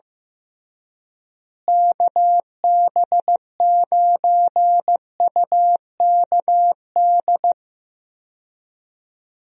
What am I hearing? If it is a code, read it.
Morse code, heard as a keyed tone, about 15 wpm, 700 Hz. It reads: KB9UKD
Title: USA Call Signs - 6 Characters - Repeated 3x - 15wpm